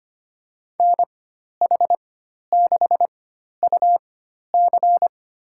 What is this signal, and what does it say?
Morse code, heard as a keyed tone, about 25 wpm, spaced as for 12 wpm, 700 Hz. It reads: NH6UC
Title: USA Call Signs - 5 Characters - 4x Character Spacing - 25wpm